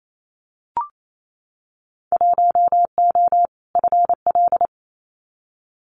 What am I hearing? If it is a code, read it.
Morse code, heard as a keyed tone, about 28 wpm, 700 Hz. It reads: 1O FL
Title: Winter Field Day Exchanges 28wpm